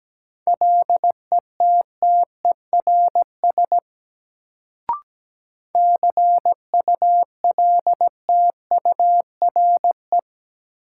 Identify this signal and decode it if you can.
Morse code, heard as a keyed tone, about 17 wpm, 700 Hz. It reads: LETTERS CULTURE
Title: Sentences from Top 1000 Words - Review Part 6 - New Word Drill - 17wpm